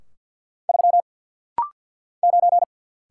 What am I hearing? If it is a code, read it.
Morse code, heard as a keyed tone, about 50 wpm, 690 Hz. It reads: 3 9
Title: Single Letter-Number Alt - Rapid-Fire 50wpm